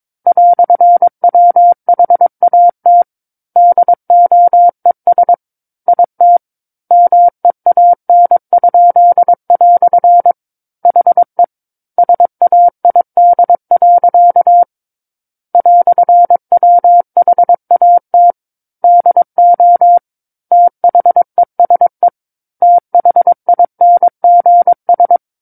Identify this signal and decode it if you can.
Morse code, heard as a keyed tone, about 22 wpm, 700 Hz. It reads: "WHAT DOES IT MEAN?" HE SAID. "WHAT DO THESE THINGS
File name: Book One / 13-Chapter 13 - 22 WPM